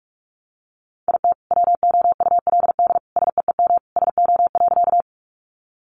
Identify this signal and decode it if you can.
Morse code, heard as a keyed tone, about 44 wpm, 700 Hz. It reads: IT WOULD SEEM SO.
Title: Sentences from Top 300 Words 45wpm